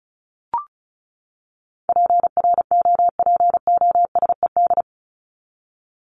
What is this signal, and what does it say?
Morse code, heard as a keyed tone, about 34 wpm, 700 Hz. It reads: PROPOSED
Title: ED Suffix - Top 100 Words with Base Words 35wpm